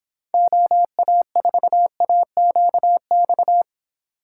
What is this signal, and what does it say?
Morse code, heard as a keyed tone, about 29 wpm, 700 Hz. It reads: OA4AQX